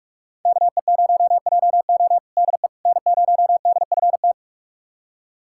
Morse code, heard as a keyed tone, about 43 wpm, 700 Hz. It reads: KE0JO DE N0DFT
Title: USA Callsign DE Callsign - 5 Characters - 45wpm